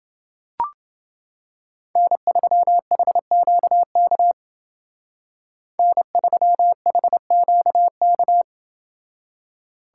Morse code, heard as a keyed tone, about 28 wpm, 700 Hz. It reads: N3HQK N3HQK